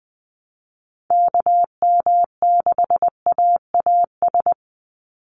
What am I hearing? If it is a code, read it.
Morse code, heard as a keyed tone, about 20 wpm, 700 Hz. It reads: KM6AAS
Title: USA Call Signs 20wpm